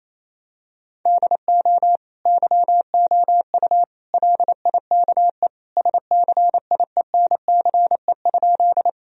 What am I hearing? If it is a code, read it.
Morse code, heard as a keyed tone, about 28 wpm, 700 Hz. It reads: DO YOU LIKE SCIENCE?
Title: Sentences from Top 1000 Words - Review Part 2 - 28wpm